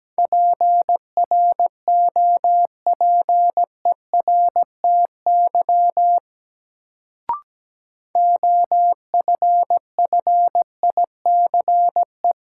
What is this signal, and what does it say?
Morse code, heard as a keyed tone, about 17 wpm, 700 Hz. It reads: PROPERTY OFFICE
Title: Sentences from Top 1000 Words - New Word Drill - 17wpm